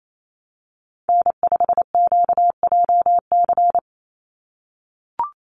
Morse code, heard as a keyed tone, about 28 wpm, 700 Hz. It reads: N5QJC
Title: USA Call Signs 28wpm